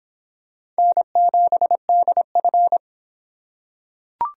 Morse code, heard as a keyed tone, about 26 wpm, 700 Hz. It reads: N7DF